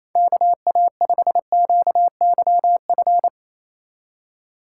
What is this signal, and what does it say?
Morse code, heard as a keyed tone, about 28 wpm, 700 Hz. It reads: KA5QYF
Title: USA Call Signs - 6 Characters - 2x Character Spacing - 28wpm